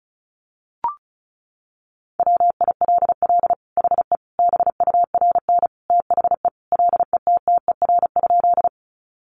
Morse code, heard as a keyed tone, about 34 wpm, 700 Hz. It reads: WILL HE BURN THE LETTER?
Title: Sentences from Top 1300 Words 35wpm